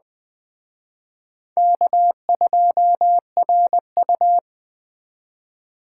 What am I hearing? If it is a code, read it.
Morse code, heard as a keyed tone, about 22 wpm, 700 Hz. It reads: K2RU